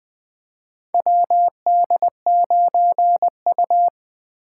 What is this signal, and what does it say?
Morse code, heard as a keyed tone, about 20 wpm, 700 Hz. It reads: WD9U
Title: USA Call Signs - 4 Characters - 2x Character Spacing - 20wpm